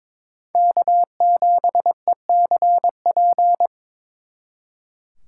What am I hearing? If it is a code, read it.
Morse code, heard as a keyed tone, about 26 wpm, 700 Hz. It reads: K7ECP
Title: Speed Racing - USA Call Signs 22wpm